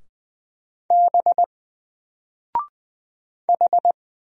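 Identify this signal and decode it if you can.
Morse code, heard as a keyed tone, about 20 wpm, 700 Hz. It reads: B H
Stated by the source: Single letter and number Morse drill at 20 wpm, from N4LCD.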